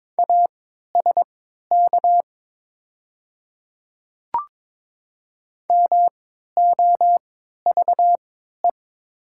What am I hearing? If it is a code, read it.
Morse code, heard as a keyed tone, about 22 wpm, spaced as for 12 wpm, 700 Hz. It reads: ASK MOVE
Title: 3x Character Spacing - Top 300 Words - 22wpm